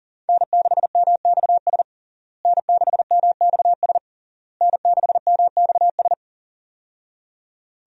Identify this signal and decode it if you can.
Morse code, heard as a keyed tone, about 40 wpm, 700 Hz. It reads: N6MXS N6MXS N6MXS
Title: USA Call Signs - 5 Characters - Repeated 3x - 40wpm